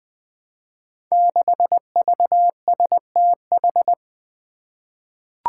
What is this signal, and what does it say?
Morse code, heard as a keyed tone, about 20 wpm, 700 Hz. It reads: 6VSTH
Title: Five Challenging Letter-Number Combinations 20wpm